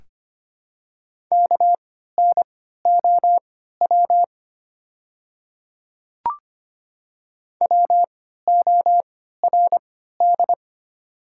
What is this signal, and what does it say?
Morse code, heard as a keyed tone, about 25 wpm, spaced as for 14 wpm, 700 Hz. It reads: KNOW WORD
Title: 3x Character Spacing - Top 100 Words - 25wpm